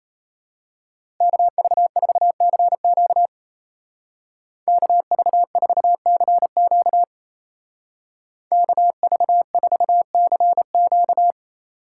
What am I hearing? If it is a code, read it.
Morse code, heard as a keyed tone, about 33 wpm, 700 Hz. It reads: KV4CQ KV4CQ KV4CQ
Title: Speed Racing - USA Call Signs 28wpm